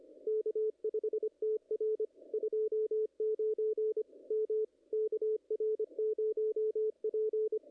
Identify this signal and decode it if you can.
Morse code, heard as a keyed tone, about 24 wpm, 430 Hz, with radio noise. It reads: K5TR 29 M KR0P
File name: contests / ss-cw-k5tr-op-n5rz -2300